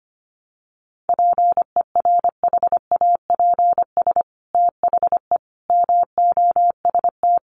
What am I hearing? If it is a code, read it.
Morse code, heard as a keyed tone, about 25 wpm, 700 Hz. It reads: PERHAPS THE MOST
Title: Sentences from Top 1300 Words 25wpm